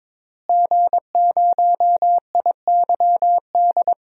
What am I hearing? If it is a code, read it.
Morse code, heard as a keyed tone, about 22 wpm, 700 Hz. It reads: G0IYD
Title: DXCC-100 Call Signs - 5 Characters - Repeated 3x - 22wpm